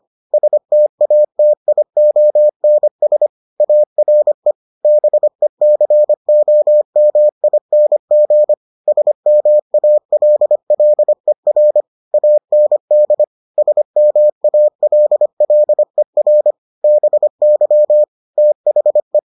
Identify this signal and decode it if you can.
Morse code, heard as a keyed tone, about 25 wpm, 600 Hz. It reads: STATIONS ARE BECOMING SMALLER AND SMALLER BY THE